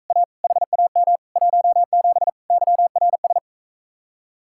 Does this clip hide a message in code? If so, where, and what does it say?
Yes: Morse code from 0.1 to 3.4 s, heard as a keyed tone, about 44 wpm, 700 Hz. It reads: A HAM 17 YRS